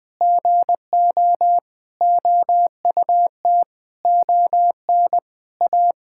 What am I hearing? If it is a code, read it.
Morse code, heard as a keyed tone, about 21 wpm, 700 Hz. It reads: GO OUT ON A
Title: Sentences with 8-Word Phrases - 20wpm